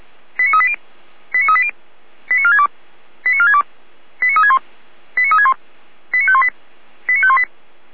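Five-tone selective call (ZVEI), heard as quick runs of five tones, tones of about 70 ms. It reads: 89380 89380 89462 89462 89361 89361 89518 89518
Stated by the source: Five-tone ZVEI selective call received on SDR, from audio.f85360.feuerwehr-orlamuende.org, a fire brigade (Feuerwehr Orlamünde) server.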